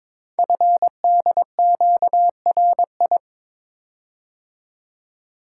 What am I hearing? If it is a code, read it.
Morse code, heard as a keyed tone, about 22 wpm, 700 Hz. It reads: FDQRI
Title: Code Groups - 5 Letters - 22wpm